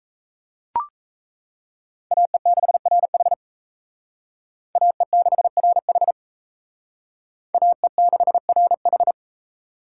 Morse code, heard as a keyed tone, about 35 wpm, 700 Hz. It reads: AE6RH AE6RH AE6RH
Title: Speed Racing - USA 2x2 Call Signs 28wpm